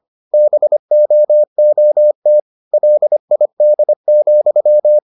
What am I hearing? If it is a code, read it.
Morse code, heard as a keyed tone, about 25 wpm, 600 Hz. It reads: BOOT LID,